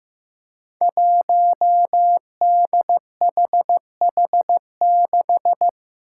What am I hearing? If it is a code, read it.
Morse code, heard as a keyed tone, about 15 wpm, 700 Hz. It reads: 1DHH6
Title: Five Challenging Letter-Number Combinations 15wpm